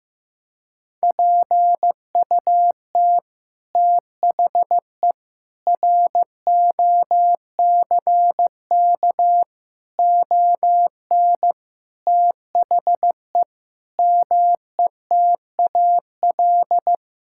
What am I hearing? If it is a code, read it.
Morse code, heard as a keyed tone, about 16 wpm, 700 Hz. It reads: PUT THE ROCK ON THE METAL
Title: Sentences from Top 1000 Words - Review Part 6 - 15wpm